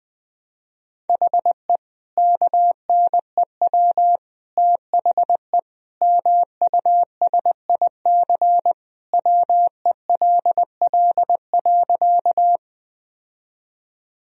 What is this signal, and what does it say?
Morse code, heard as a keyed tone, about 20 wpm, 700 Hz. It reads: HE KNEW THE MUSIC WELL.
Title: Sentences from Top 1000 Words - Review Part 1 - 20wpm